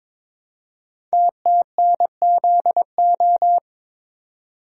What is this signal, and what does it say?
Morse code, heard as a keyed tone, about 22 wpm, 700 Hz. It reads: TTNZO
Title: Code Groups - 5 Letters - 22wpm